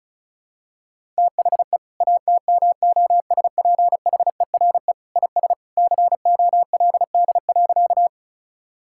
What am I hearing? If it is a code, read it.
Morse code, heard as a keyed tone, about 34 wpm, 700 Hz. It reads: THE ATMOSPHERE IS COLD.
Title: Sentences from Top 1600 Words 35wpm